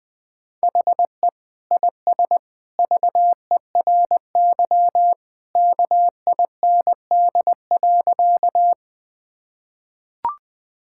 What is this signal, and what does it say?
Morse code, heard as a keyed tone, about 20 wpm, 700 Hz. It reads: HE IS VERY KIND.
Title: Sentences with 2-Word Phrases - Spoken Beginning - 20wpm